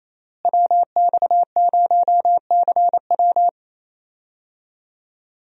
Morse code, heard as a keyed tone, about 28 wpm, 700 Hz. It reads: WX0CW
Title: USA Call Signs - 5 Characters - Repeated 3x - 28wpm